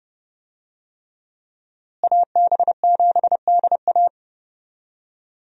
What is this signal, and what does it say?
Morse code, heard as a keyed tone, about 30 wpm, 700 Hz. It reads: AB7DA
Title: USA Call Signs - 5 Characters - Repeated 3x - 30wpm